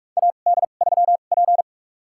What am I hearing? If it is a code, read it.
Morse code, heard as a keyed tone, about 44 wpm, spaced as for 32 wpm, 700 Hz. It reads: AD3P
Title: USA Call Signs - 4 Characters - 2x Character Spacing - 45wpm